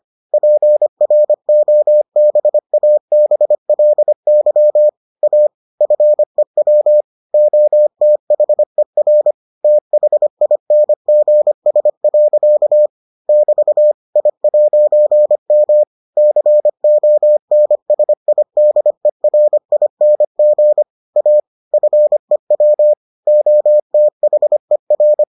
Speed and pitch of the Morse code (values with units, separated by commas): 25 wpm, 600 Hz